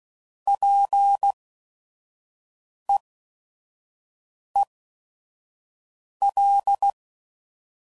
Morse code, heard as a keyed tone, about 16 wpm, spaced as for 5 wpm, 800 Hz. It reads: PEEL
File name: Prefixes / By First Letter / Ps